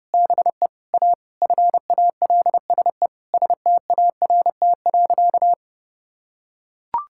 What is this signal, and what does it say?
Morse code, heard as a keyed tone, about 30 wpm, 700 Hz. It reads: BE A FALSE START.
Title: Sentences from Top 1600 Words 30wpm